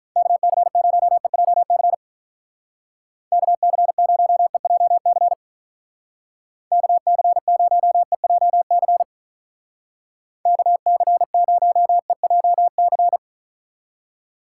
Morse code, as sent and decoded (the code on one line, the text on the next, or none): -.- -.-. ----- . .--- -.-. / -.- -.-. ----- . .--- -.-. / -.- -.-. ----- . .--- -.-. / -.- -.-. ----- . .--- -.-.
KC0EJC KC0EJC KC0EJC KC0EJC